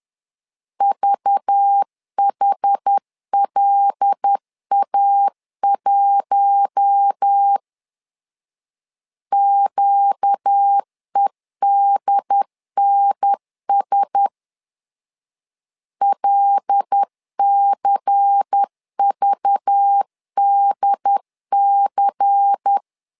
Morse code, heard as a keyed tone, about 10 wpm, 800 Hz. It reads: VHLA1 QEDNS LCVDC
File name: LEZIONI / 5 LBVC / 5 LBVC 10wpm 2349